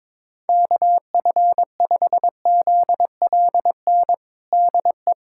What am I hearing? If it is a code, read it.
Morse code, heard as a keyed tone, about 22 wpm, 700 Hz. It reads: KF5ZLN DE
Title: USA Callsign DE Callsign 22wpm